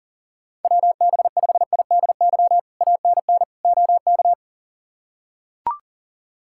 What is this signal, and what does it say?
Morse code, heard as a keyed tone, about 40 wpm, 700 Hz. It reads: WB5IDY ANN OK
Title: K1USN SST QSO Practice - 40wpm